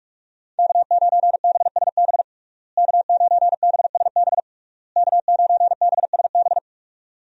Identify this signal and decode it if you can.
Morse code, heard as a keyed tone, about 46 wpm, 700 Hz. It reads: K9BSB K9BSB K9BSB
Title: USA Call Signs - 5 Characters - Repeated 3x - 45wpm